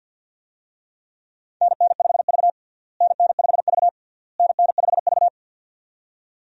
Morse code, heard as a keyed tone, about 51 wpm, 700 Hz. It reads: NN5V NN5V NN5V
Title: USA Call Signs - 4 Characters - Repeated 3x - 50wpm